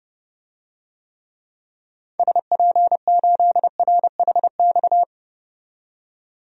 Morse code, heard as a keyed tone, about 30 wpm, 700 Hz. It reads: SP8RHX
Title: DXCC-100 Call Signs - 6 Characters - 30wpm